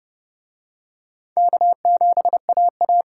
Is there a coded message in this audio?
Yes: Morse code from 1.4 to 3.0 s, heard as a keyed tone, about 29 wpm, 700 Hz. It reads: K7AA